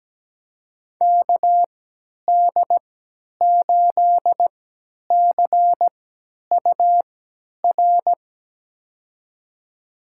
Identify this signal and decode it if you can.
Morse code, heard as a keyed tone, about 17 wpm, spaced as for 10 wpm, 700 Hz. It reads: KD8CUR
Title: USA Call Signs - 3x Character Spacing - 17wpm